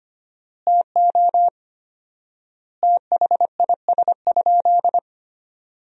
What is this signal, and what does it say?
Morse code, heard as a keyed tone, about 25 wpm, 700 Hz. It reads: TO THIS?
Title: 4x Word-Spacing - Sentences from Top 100 Words - 25wpm